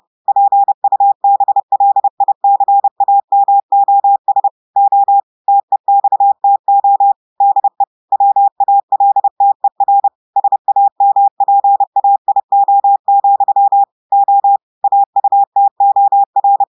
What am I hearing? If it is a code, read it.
Morse code, heard as a keyed tone, about 30 wpm, 800 Hz. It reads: PUBLICAMOS O TEXTO DE WALTER SAMPAIO, O AUTOR
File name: audios / portugais / portugais texte en clair 1 vitesse 30 mots minutes